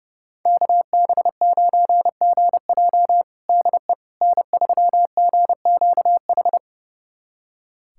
Morse code, heard as a keyed tone, about 31 wpm, 700 Hz. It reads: KB9GJ DE N3GQH
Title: USA Callsign DE Callsign - 5 Characters - 30wpm